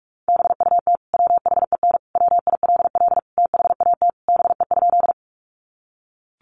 Morse code, heard as a keyed tone, about 44 wpm, 700 Hz. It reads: BUT WHEN WILL THAT BE?
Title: Sentences from Top 100 Words 45wpm